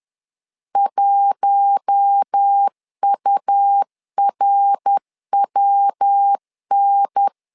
Morse code, heard as a keyed tone, about 10 wpm, 800 Hz. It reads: 1URWN
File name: LEZIONI / 4 Q1S 10wpm 2133